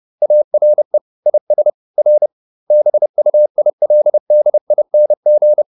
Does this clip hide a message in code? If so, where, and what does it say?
Yes: Morse code from 0.2 to 5.6 s, heard as a keyed tone, about 30 wpm, 600 Hz. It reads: ARE IS R BUILDING